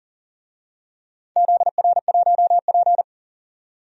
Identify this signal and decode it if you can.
Morse code, heard as a keyed tone, about 40 wpm, 700 Hz. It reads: ZR1P
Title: DXCC-100 Call Signs - 4 Characters - 40wpm